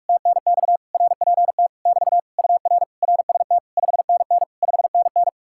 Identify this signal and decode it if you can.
Morse code, heard as a keyed tone, about 66 wpm, spaced as for 55 wpm, 700 Hz. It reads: TNX RPT <BT> UR RST 5NN 5NN
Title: Speed-Racing Common QSO Phrases Encore - 45wpm